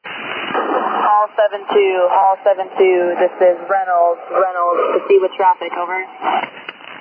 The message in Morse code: none

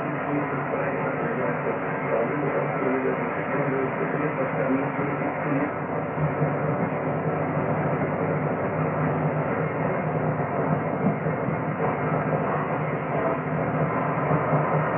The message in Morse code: none

